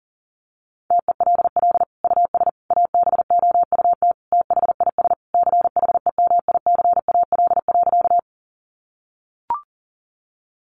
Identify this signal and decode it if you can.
Morse code, heard as a keyed tone, about 40 wpm, 700 Hz. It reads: TELL US ABOUT THIS CHEMICAL.
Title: Sentences from Top 1200 Words 40wpm